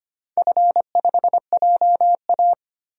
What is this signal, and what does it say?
Morse code, heard as a keyed tone, about 25 wpm, 700 Hz. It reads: F5JA
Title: DXCC-100 Call Signs - 4 Characters - 25wpm